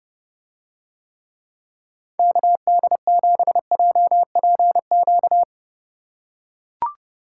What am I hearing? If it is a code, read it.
Morse code, heard as a keyed tone, about 31 wpm, 700 Hz. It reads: KD7JPQ